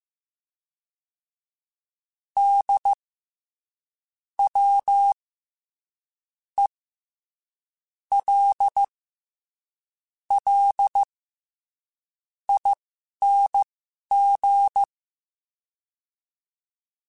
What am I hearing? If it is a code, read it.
Morse code, heard as a keyed tone, about 15 wpm, spaced as for 6 wpm, 800 Hz. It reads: DWELLING
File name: Special INGs-mp3s / Special INGs-mp3s / SlowToFaster ing @10